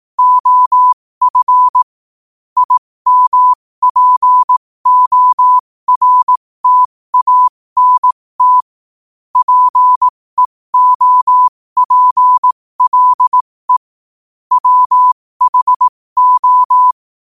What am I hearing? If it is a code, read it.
Morse code, heard as a keyed tone, about 18 wpm, spaced as for 15 wpm, 1000 Hz. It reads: OF IMPORTANT PEOPLE WHO
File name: qotdfiles / 15wpm